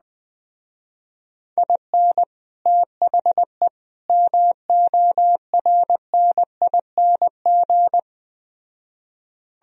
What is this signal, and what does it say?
Morse code, heard as a keyed tone, about 20 wpm, 700 Hz. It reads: IN THE MORNING
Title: Speed-Racing - Sets of 3 Words 17wpm